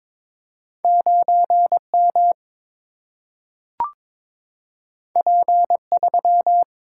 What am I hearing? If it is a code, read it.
Morse code, heard as a keyed tone, about 22 wpm, 700 Hz. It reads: 9M P3
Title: Single-Letter Single-Number 22wpm